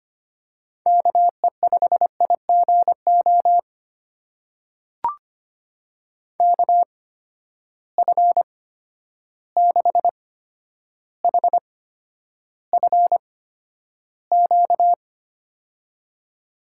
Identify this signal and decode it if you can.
Morse code, heard as a keyed tone, about 25 wpm, spaced as for 11 wpm, 700 Hz. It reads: KE5IGO KF6HFQ
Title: USA Call Signs - 6 Characters - 8x Character Spacing - 25wpm